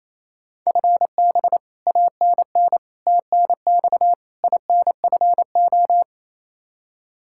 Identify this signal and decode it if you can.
Morse code, heard as a keyed tone, about 28 wpm, 700 Hz. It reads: FB ANN TNX INFO
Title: Practice QSO - Exchange 3 - More Info - 28wpm